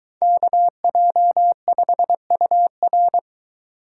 Morse code, heard as a keyed tone, about 24 wpm, 700 Hz. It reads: KJ5UR